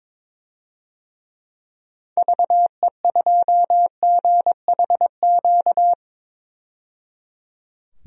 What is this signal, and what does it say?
Morse code, heard as a keyed tone, about 22 wpm, 700 Hz. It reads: VE2GHQ
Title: DXCC-100 Call Signs - 6 Characters - 22wpm